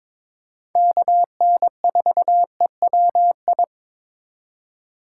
Morse code, heard as a keyed tone, about 22 wpm, 700 Hz. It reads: KN4EWI
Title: K1USN SST QSO Practice - 22wpm